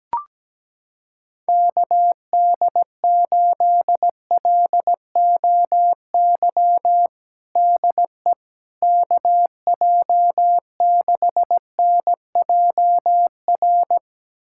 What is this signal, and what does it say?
Morse code, heard as a keyed tone, about 17 wpm, 700 Hz. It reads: KD8LOY DE KJ6NJR
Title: USA Callsign DE Callsign - 6 Characters - 17wpm